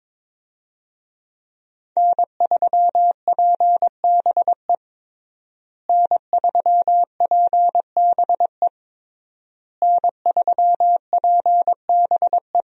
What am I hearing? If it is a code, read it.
Morse code, heard as a keyed tone, about 22 wpm, 700 Hz. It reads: N3PBE N3PBE N3PBE
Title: USA Call Signs - 5 Characters - Repeated 3x - 22wpm